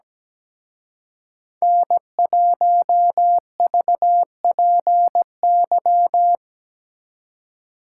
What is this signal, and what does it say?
Morse code, heard as a keyed tone, about 21 wpm, 700 Hz. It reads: N1VPY
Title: Speed Racing - USA 1x3 Call Signs 17wpm